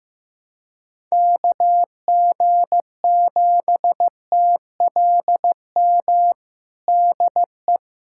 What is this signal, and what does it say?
Morse code, heard as a keyed tone, about 15 wpm, 700 Hz. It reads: KG7TLM DE
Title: USA Callsign DE Callsign - 6 Characters - 15wpm